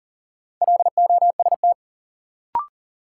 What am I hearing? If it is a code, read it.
Morse code, heard as a keyed tone, about 38 wpm, 700 Hz. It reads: LOST
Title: Sentences from Top 1400 Words - New Word Drill - 40wpm